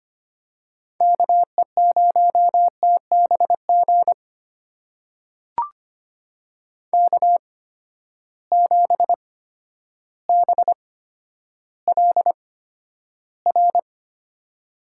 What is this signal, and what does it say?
Morse code, heard as a keyed tone, about 25 wpm, spaced as for 11 wpm, 700 Hz. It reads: KE0TBG K7BLR
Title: USA Call Signs - 8x Character Spacing - 25wpm